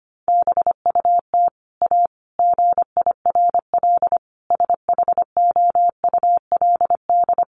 Morse code, heard as a keyed tone, about 25 wpm, 700 Hz. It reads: BUT A GIRL SHOULD
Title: Sentences from Top 1200 Words 25wpm